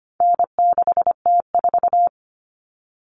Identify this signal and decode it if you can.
Morse code, heard as a keyed tone, about 25 wpm, 700 Hz. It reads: N6T4
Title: All Cut Numbers - 4-Digit 25wpm